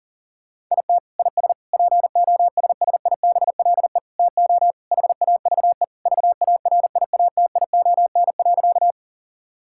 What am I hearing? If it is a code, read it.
Morse code, heard as a keyed tone, about 40 wpm, 700 Hz. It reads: IT IS POSSIBLE TO HAVE VARIATION.
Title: Sentences from Top 1600 Words 40wpm